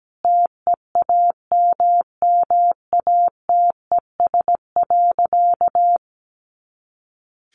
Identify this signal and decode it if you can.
Morse code, heard as a keyed tone, about 17 wpm, 700 Hz. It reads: TEAMMATES.
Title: Easy Phrases 17wpm